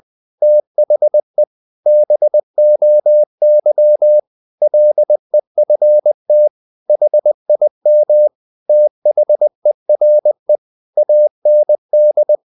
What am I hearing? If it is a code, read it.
Morse code, heard as a keyed tone, about 20 wpm, 600 Hz. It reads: THE BOY LEFT HIM THERE AND
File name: the old man and the sea / the old man and the sea part 1 20wpm